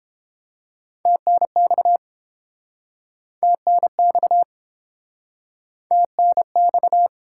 Morse code, heard as a keyed tone, about 32 wpm, 700 Hz. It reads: TNX TNX TNX